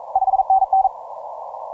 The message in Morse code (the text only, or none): 5NN